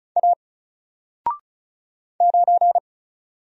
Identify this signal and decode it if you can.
Morse code, heard as a keyed tone, about 36 wpm, 700 Hz. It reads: A 9
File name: Single-Letter-Number-35wpm